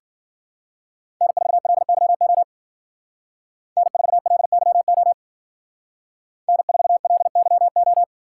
Morse code, heard as a keyed tone, about 50 wpm, 700 Hz. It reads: N4LYO N4LYO N4LYO
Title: Speed Racing - USA 1x3 Call Signs 40wpm